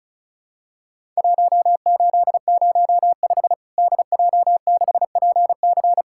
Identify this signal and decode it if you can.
Morse code, heard as a keyed tone, about 34 wpm, 700 Hz. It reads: 1805 DJ6PC
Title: Worked All Europe DX Contest - QTC Exchange - 35wpm